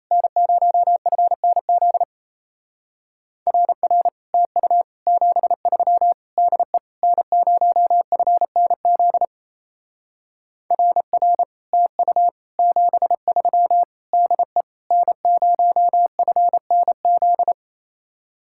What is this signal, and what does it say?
Morse code, heard as a keyed tone, about 36 wpm, 700 Hz. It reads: N0FNZ RR TU 73 DE N0FNZ RR TU 73 DE N0FNZ